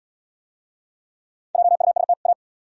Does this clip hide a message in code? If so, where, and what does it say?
Yes: Morse code from 1.6 to 2.3 s, heard as a keyed tone, about 71 wpm, 700 Hz. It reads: JUST A